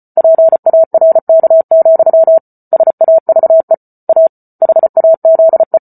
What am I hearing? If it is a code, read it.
Morse code, heard as a keyed tone, about 34 wpm, 650 Hz. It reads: PARK, SAVE A HAZE
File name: Book Two / 08-Chapter 8 - 34 WPM